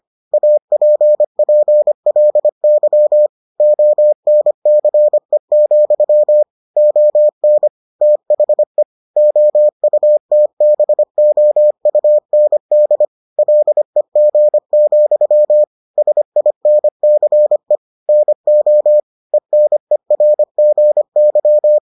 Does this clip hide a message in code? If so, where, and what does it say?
Yes: Morse code from 0.3 to 21.8 s, heard as a keyed tone, about 25 wpm, 600 Hz. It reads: APPLY ONCE, ON THE OUTBOUND LEG, SINCE NO ENERGY